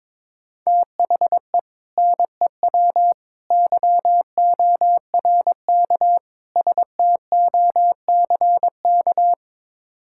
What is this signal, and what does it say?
Morse code, heard as a keyed tone, about 22 wpm, 700 Hz. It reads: THE NEW YORK STOCK
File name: practice / 22wpm / Build A Phrase - 5-Word Phrases - 22wpm